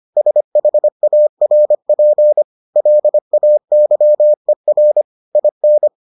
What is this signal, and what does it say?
Morse code, heard as a keyed tone, about 25 wpm, 600 Hz. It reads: SHARP LAYER IN